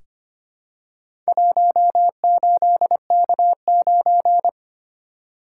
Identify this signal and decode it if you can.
Morse code, heard as a keyed tone, about 25 wpm, 690 Hz. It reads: 18K9
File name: practice / 25wpm / Four Letter-Number Combinations - 25wpm